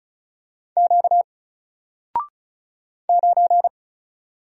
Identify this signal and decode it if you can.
Morse code, heard as a keyed tone, about 34 wpm, 700 Hz. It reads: Q 9
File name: Single-Letter-Number-35wpm